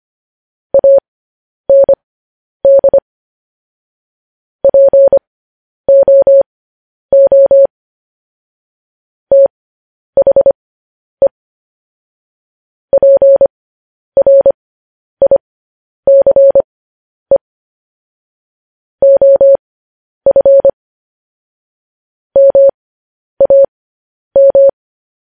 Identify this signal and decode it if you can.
Morse code, heard as a keyed tone, about 25 wpm, spaced as for 10 wpm, 550 Hz. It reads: AND POO THE PRICE OF MAM
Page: news-headlines-in-morse-code-at-10-wpm